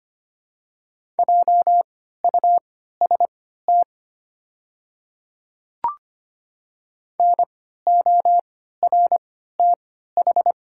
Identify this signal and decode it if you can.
Morse code, heard as a keyed tone, about 25 wpm, spaced as for 14 wpm, 690 Hz. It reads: JUST NORTH
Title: 3x Character Spacing - Top 500 Words - 25wpm